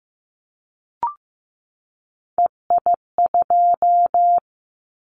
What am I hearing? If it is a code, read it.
Morse code, heard as a keyed tone, about 15 wpm, 700 Hz. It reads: EI2
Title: Callsign Prefixes for Top 50 DXCC - 15wpm